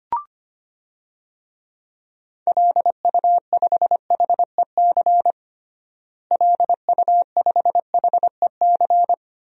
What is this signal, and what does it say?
Morse code, heard as a keyed tone, about 25 wpm, 700 Hz. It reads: LU5HEC LU5HEC
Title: DXCC-100 Call Signs - 6 Characters - Repeated 2x - 25wpm